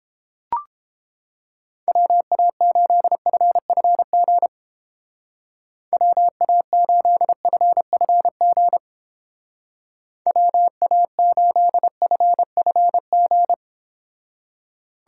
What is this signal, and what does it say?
Morse code, heard as a keyed tone, about 31 wpm, 700 Hz. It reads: WA8FFG WA8FFG WA8FFG